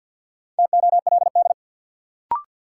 Morse code, heard as a keyed tone, about 53 wpm, 700 Hz. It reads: TOLD
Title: Sentences from Top 400 Words - New Word Drill - 50wpm